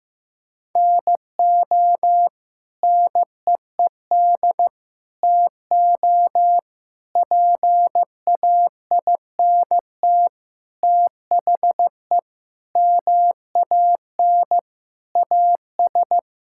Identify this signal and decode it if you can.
Morse code, heard as a keyed tone, about 15 wpm, 700 Hz. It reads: NO NEED TO PAINT THE MAN AS
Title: Sentences from Top 1000 Words - Review Part 6 - 15wpm